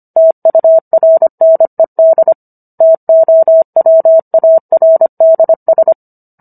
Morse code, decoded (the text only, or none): TURNED TOWARDS